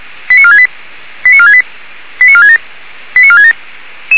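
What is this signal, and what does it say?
Five-tone selective call (ZVEI), heard as quick runs of five tones, tones of about 80 ms. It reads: 89468 89468 89467 89467
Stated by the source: ZVEI five-tone selective call received on SDR, fire brigade alerting.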